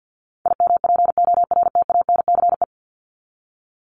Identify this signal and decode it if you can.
Morse code, heard as a keyed tone, about 50 wpm, 700 Hz. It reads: IMPORTANCE
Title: Top 1001-1250 Words 50wpm